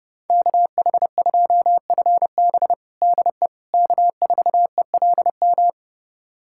Morse code, heard as a keyed tone, about 29 wpm, 700 Hz. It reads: KH2FB DE K4ELM
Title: USA Callsign DE Callsign - 5 Characters - 30wpm